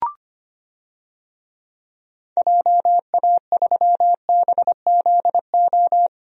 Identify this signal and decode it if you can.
Morse code, heard as a keyed tone, about 25 wpm, 690 Hz. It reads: JA3BZO
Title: DXCC-100 Call Signs - 6 Characters - 25wpm